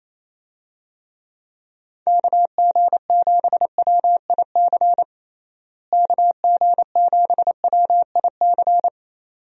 Morse code, heard as a keyed tone, about 28 wpm, 700 Hz. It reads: KG7WIC KG7WIC
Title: USA Call Signs - 6 Characters - Repeated 2x - 28wpm